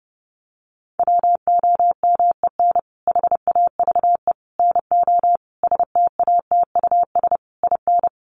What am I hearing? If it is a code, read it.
Morse code, heard as a keyed tone, about 30 wpm, 700 Hz. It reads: WOMEN HAVE NO STATUS IN
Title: Sentences from Top 1400 Words 30wpm